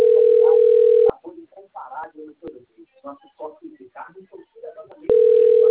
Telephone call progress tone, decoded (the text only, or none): ringback tone